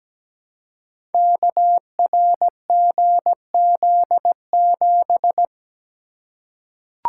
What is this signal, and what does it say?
Morse code, heard as a keyed tone, about 17 wpm, 700 Hz. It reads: KRGZ7
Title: Code Groups - 5 Letter:Numbers - 17wpm